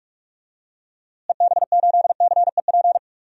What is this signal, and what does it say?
Morse code, heard as a keyed tone, about 41 wpm, 700 Hz. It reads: EB8CEP
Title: DXCC-100 Call Signs - 6 Characters - Repeated 3x - 45wpm